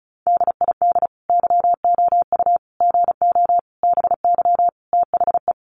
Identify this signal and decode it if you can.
Morse code, heard as a keyed tone, about 34 wpm, 700 Hz. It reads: DID YOU GO BY THE
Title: Sentences from Top 1200 Words 35wpm